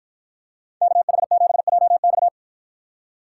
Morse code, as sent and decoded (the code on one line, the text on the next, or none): -.- .... --... .--- -..-
KH7JX